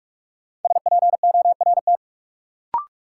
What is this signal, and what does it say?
Morse code, heard as a keyed tone, about 46 wpm, 700 Hz. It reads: SPORT
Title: Sentences from Top 1500 Words - New Word Drill Alt - 45wpm